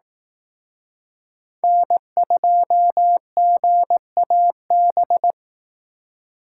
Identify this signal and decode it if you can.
Morse code, heard as a keyed tone, about 18 wpm, 700 Hz. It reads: N2GAB
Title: Speed Racing - USA Call Signs 15wpm